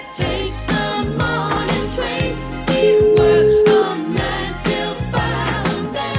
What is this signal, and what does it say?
Ringback tone (Europe, 425 Hz), with noise.